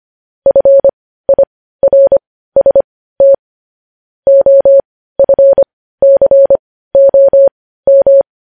Morse code, heard as a keyed tone, about 25 wpm, spaced as for 15 wpm, 550 Hz. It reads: FIRST OFCOM
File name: podcast / News headlines-1x-25x15